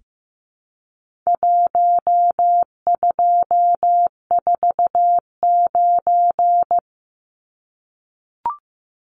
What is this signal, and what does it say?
Morse code, heard as a keyed tone, about 15 wpm, 700 Hz. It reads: 1249
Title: Numbers - 4-Digit 15wpm